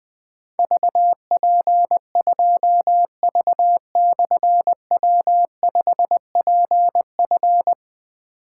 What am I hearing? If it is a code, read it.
Morse code, heard as a keyed tone, about 21 wpm, 700 Hz. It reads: VP2V/W5PF
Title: Difficult Call Signs 20wpm